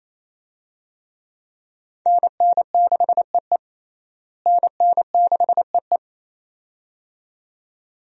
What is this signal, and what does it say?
Morse code, heard as a keyed tone, about 28 wpm, 700 Hz. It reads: NN6EE NN6EE
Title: USA Call Signs - 5 Characters - Repeated 2x - 28wpm